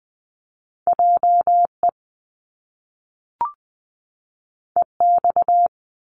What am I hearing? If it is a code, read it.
Morse code, heard as a keyed tone, about 20 wpm, 700 Hz. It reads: J E E X
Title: Top Two Letter Combinations in Words 20wpm